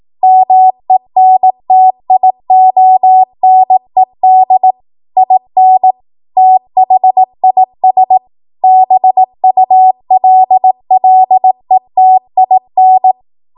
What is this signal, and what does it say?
Morse code, heard as a keyed tone, about 18 wpm, 750 Hz. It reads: MENTIONED IN THIS BULLETIN
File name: Morse / 18WPM